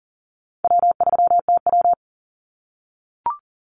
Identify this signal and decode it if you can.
Morse code, heard as a keyed tone, about 40 wpm, 700 Hz. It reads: W3TW
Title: Call Signs - 40wpm